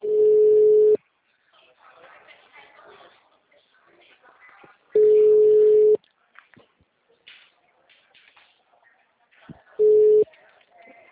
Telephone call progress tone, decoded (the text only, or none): ringback tone